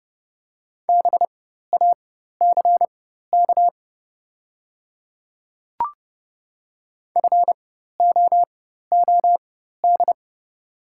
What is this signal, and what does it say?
Morse code, heard as a keyed tone, about 29 wpm, spaced as for 14 wpm, 700 Hz. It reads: BACK FOOD
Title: 4x Character Spacing - Top 300 Words - 30wpm